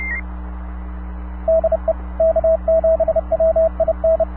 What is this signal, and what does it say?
Morse code, heard as a keyed tone, about 30 wpm, 650 Hz, with radio noise. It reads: DE K7WIN
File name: sat / AO-51 14Jul2010 SSTV